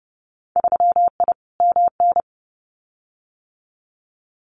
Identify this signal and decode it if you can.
Morse code, heard as a keyed tone, about 30 wpm, 700 Hz. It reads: 3I MN
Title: Winter Field Day Exchanges 30wpm